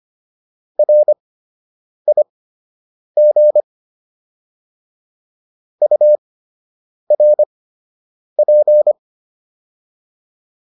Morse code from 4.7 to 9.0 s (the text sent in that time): URP